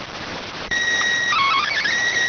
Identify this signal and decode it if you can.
SSTV (slow-scan television), PD 290.